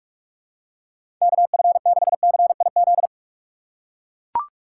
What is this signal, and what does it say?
Morse code, heard as a keyed tone, about 42 wpm, 700 Hz. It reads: KF6CIZ
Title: USA Call Signs - 6 Characters - 3x Character Spacing - 45wpm